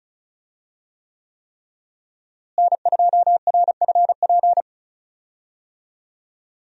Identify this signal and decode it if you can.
Morse code, heard as a keyed tone, about 34 wpm, 700 Hz. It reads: N2RFP
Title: USA Call Signs - 5 Characters - Repeated 3x - 35wpm